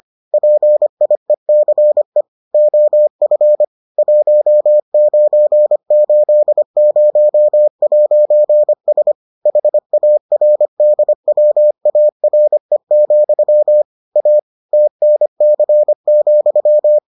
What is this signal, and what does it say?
Morse code, heard as a keyed tone, about 25 wpm, 600 Hz. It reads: PIECE OF 1980'S HARDWARE, A TNC,